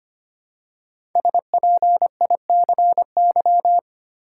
Morse code, heard as a keyed tone, about 25 wpm, 700 Hz. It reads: SPICY
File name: 5 Letter Words - First 3 Letters Given - 25wpm